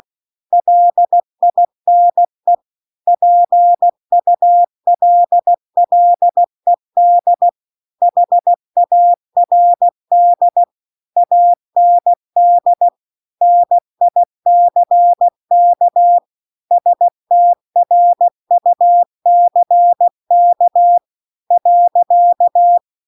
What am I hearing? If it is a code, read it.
Morse code, heard as a keyed tone, about 16 wpm, 700 Hz. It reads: LINE PULLED HARD AND NICK STRUCK .